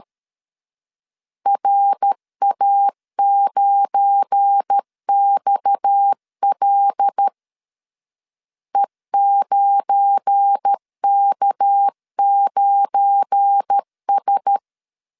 Morse code, heard as a keyed tone, about 13 wpm, 800 Hz. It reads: RA9XL E9K9S
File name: LEZIONI / 7 GF9Z / 7 GF9 12wpm 2319